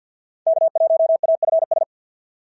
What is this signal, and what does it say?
Morse code, heard as a keyed tone, about 50 wpm, 650 Hz. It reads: K1AFS
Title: SOTA QSO Practice 50wpm